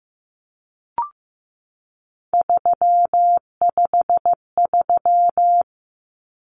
Morse code, heard as a keyed tone, about 15 wpm, 700 Hz. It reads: 353
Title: Common Cut Numbers - 3-Digit 15wpm